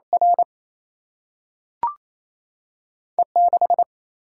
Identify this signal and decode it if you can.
Morse code, heard as a keyed tone, about 28 wpm, 700 Hz. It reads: R E 6
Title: Single-Letter Single-Number 28wpm